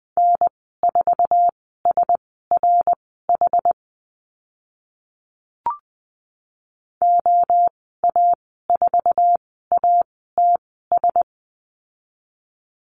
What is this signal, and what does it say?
Morse code, heard as a keyed tone, about 20 wpm, spaced as for 14 wpm, 700 Hz. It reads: N4SRH OA4ATS
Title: 088 Lesson 15g - Number 5 - Call Signs 20wpm